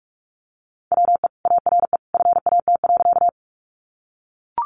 Morse code, heard as a keyed tone, about 44 wpm, 700 Hz. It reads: WE ARE FAT.
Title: Sentences from Top 600 Words 45wpm